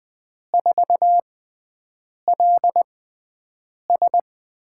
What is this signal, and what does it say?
Morse code, heard as a keyed tone, about 20 wpm, spaced as for 7 wpm, 700 Hz. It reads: 4LS
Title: USA Call Signs - 5 Characters - 6x Character Spacing - 20wpm